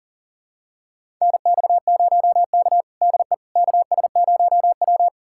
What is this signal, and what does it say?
Morse code, heard as a keyed tone, about 40 wpm, 700 Hz. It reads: NX0K DE KS0W
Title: USA Callsign DE Callsign - 4 Characters - 40wpm